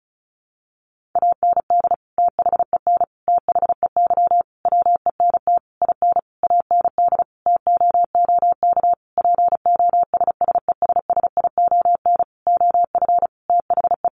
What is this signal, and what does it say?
Morse code, heard as a keyed tone, about 34 wpm, 700 Hz. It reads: AND THEN THEY WENT IN AND TOOK POSSESSION OF THE
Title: Sentences from Top 1400 Words 35wpm